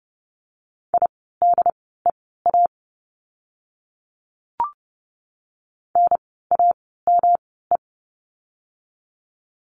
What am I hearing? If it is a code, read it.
Morse code, heard as a keyed tone, about 30 wpm, spaced as for 17 wpm, 700 Hz. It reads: IDEA NAME
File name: course / 30:10wpm / 108 Lesson 18f - Letter m - Words 30wpm